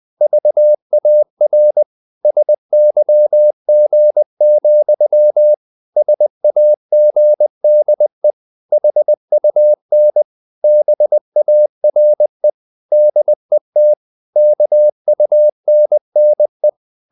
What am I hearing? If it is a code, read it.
Morse code, heard as a keyed tone, about 20 wpm, 600 Hz. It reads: VAR SYG, SAGDE HUN BARE DET KUNNE
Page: Ferie 20 wpm